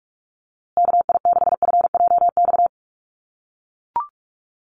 Morse code, heard as a keyed tone, about 44 wpm, 700 Hz. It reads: KI6FJX
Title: USA Call Signs Repeated 3x 45wpm